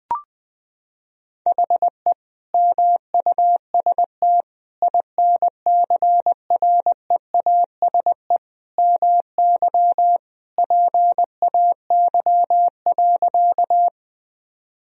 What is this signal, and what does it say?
Morse code, heard as a keyed tone, about 20 wpm, 700 Hz. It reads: HE MUST INCREASE MY PAY.
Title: Sentences from Top 1000 Words - Review Part 4 - 20wpm